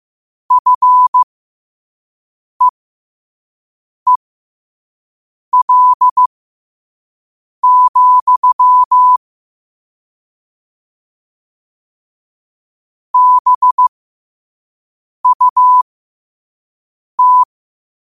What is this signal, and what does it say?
Morse code, heard as a keyed tone, about 15 wpm, spaced as for 5 wpm, 1000 Hz. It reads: FEEL, BUT
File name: qotdfiles / 5wpm